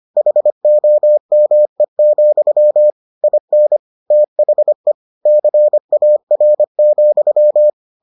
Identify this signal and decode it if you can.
Morse code, heard as a keyed tone, about 25 wpm, 600 Hz. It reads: HOME, IN THE CAR,